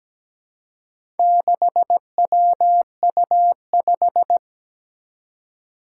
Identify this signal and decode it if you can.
Morse code, heard as a keyed tone, about 17 wpm, 700 Hz. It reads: 6WU5
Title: Four Challenging Letter-Number Combinations 17wpm